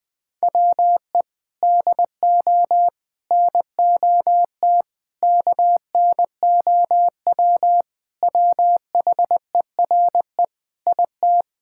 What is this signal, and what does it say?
Morse code, heard as a keyed tone, about 21 wpm, 700 Hz. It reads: WE DO NOT KNOW WHERE IT
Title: Sentences with 4-Word Phrases - 20wpm